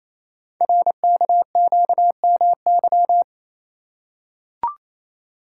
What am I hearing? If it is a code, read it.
Morse code, heard as a keyed tone, about 28 wpm, 700 Hz. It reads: RKQMY
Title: Code Groups - 5 Letter:Numbers - 28wpm